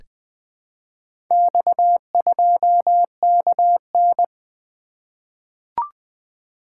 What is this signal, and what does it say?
Morse code, heard as a keyed tone, about 20 wpm, 700 Hz. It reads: X2KN
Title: Four Letter-Number Combinations - 20wpm